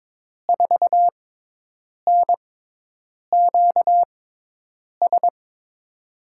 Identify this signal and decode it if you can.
Morse code, heard as a keyed tone, about 22 wpm, spaced as for 8 wpm, 700 Hz. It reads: 4NQS
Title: USA Call Signs - 6 Characters - 6x Character Spacing - 22wpm